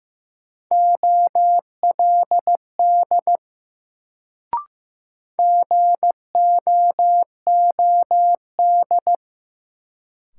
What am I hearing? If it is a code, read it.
Morse code, heard as a keyed tone, about 15 wpm, 700 Hz. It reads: OLD GOOD
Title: Sentences from Top 200 Words - New Word Drill - 15wpm